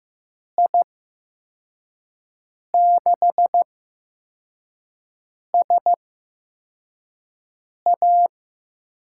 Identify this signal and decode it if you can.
Morse code, heard as a keyed tone, about 15 wpm, spaced as for 4 wpm, 700 Hz. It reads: I6SA
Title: USA Call Signs - 6 Characters - 8x Character Spacing - 15wpm